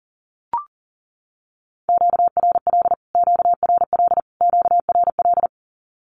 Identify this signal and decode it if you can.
Morse code, heard as a keyed tone, about 40 wpm, 700 Hz. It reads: QRL QRL QRL
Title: Q-Codes Repeated 3x 40wpm